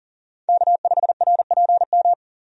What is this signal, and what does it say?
Morse code, heard as a keyed tone, about 40 wpm, 700 Hz. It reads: K5RPM
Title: USA Call Signs - 5 Characters - 40wpm